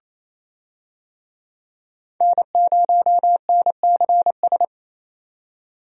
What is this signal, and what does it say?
Morse code, heard as a keyed tone, about 28 wpm, 700 Hz. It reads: N0NCS
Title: USA Call Signs - 5 Characters - 28wpm